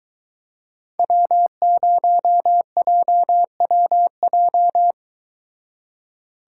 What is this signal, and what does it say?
Morse code, heard as a keyed tone, about 22 wpm, 700 Hz. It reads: W0JWJ